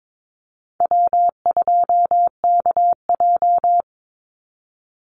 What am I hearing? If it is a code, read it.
Morse code, heard as a keyed tone, about 22 wpm, 700 Hz. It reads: W2KJ
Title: Easy Call Signs - 22wpm